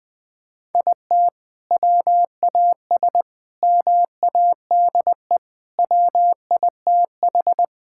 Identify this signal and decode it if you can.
Morse code, heard as a keyed tone, about 19 wpm, 700 Hz. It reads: IT WAS MADE WITH
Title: Sentences from Top 1000 Words - Review Part 5 - 20wpm